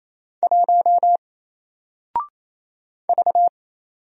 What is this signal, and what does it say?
Morse code, heard as a keyed tone, about 27 wpm, 700 Hz. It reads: 1 V
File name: Single-Letter-Number-28wpm